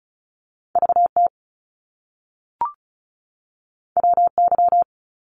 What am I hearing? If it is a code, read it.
Morse code, heard as a keyed tone, about 36 wpm, 700 Hz. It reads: VT WY
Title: US States 35wpm